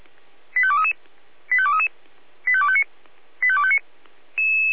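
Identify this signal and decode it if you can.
Five-tone selective call (ZVEI), heard as quick runs of five tones, tones of about 70 ms. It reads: 85233 85233 85259 85259
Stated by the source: ZVEI five-tone selective call received on an SDR.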